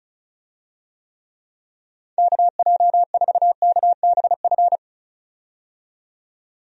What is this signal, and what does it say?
Morse code, heard as a keyed tone, about 36 wpm, 700 Hz. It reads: KJ4KBF
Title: USA Call Signs - 6 Characters - 35wpm